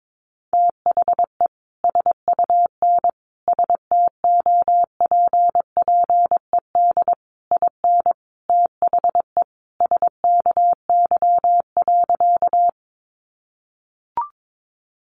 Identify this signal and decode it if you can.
Morse code, heard as a keyed tone, about 22 wpm, 700 Hz. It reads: THE SUN STOPPED IN THE SKY.
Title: A Sound Of Thunder 22wpm